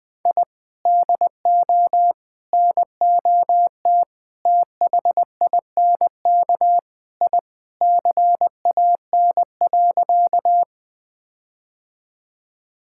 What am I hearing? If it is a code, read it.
Morse code, heard as a keyed tone, about 20 wpm, 700 Hz. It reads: I DO NOT THINK I CAN.
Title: Sentences with 4-Word Phrases - 20wpm